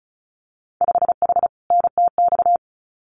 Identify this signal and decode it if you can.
Morse code, heard as a keyed tone, about 34 wpm, 700 Hz. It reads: 5H NTX
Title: Winter Field Day Exchanges Encore 35wpm